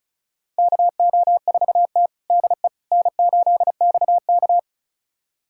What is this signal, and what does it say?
Morse code, heard as a keyed tone, about 36 wpm, 700 Hz. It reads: KO4T DE N8XK
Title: USA Callsign DE Callsign - 4 Characters - 35wpm